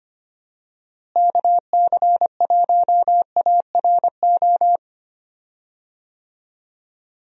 Morse code, heard as a keyed tone, about 25 wpm, 700 Hz. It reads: KC1ARO
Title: USA Call Signs - 6 Characters - Repeated 3x - 25wpm